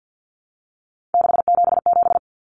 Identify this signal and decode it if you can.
Morse code, heard as a keyed tone, about 50 wpm, 700 Hz. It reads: 677
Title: Numbers - 3-Digit 50wpm